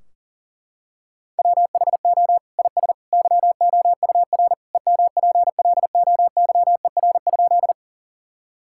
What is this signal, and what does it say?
Morse code, heard as a keyed tone, about 40 wpm, 700 Hz. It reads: WHO IS YOUR EMPLOYER?
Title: Sentences from Top 1600 Words 40wpm